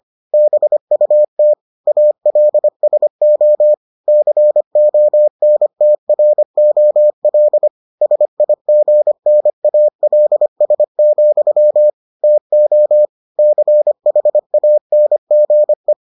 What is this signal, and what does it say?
Morse code, heard as a keyed tone, about 25 wpm, 600 Hz. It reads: BUT ALSO CONTROL SIGNALS, TO CHANGE